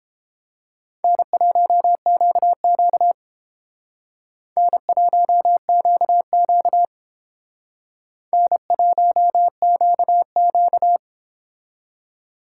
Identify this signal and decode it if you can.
Morse code, heard as a keyed tone, about 28 wpm, 700 Hz. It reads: N1QQ N1QQ N1QQ